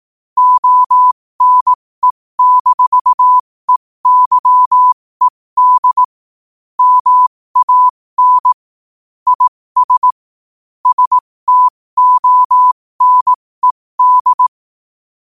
Morse code, heard as a keyed tone, about 18 wpm, spaced as for 15 wpm, 1000 Hz. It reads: ONE-EYED MAN IS STONED
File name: qotdfiles / 15wpm